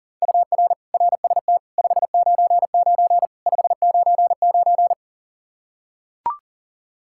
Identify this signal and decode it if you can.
Morse code, heard as a keyed tone, about 40 wpm, 700 Hz. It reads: UR RST 599 599
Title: Practice QSO - Exchange 1 - Beginning - 40wpm